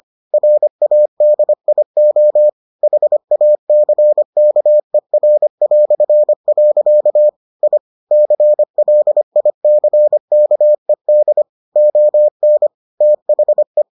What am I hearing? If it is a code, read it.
Morse code, heard as a keyed tone, about 25 wpm, 600 Hz. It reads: RADIO HACKER". I CLICKED ON THE